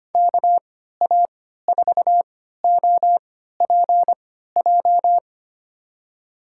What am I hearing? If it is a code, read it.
Morse code, heard as a keyed tone, about 25 wpm, spaced as for 14 wpm, 700 Hz. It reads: KA4OPJ
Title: USA Call Signs - 6 Characters - 3x Character Spacing - 25wpm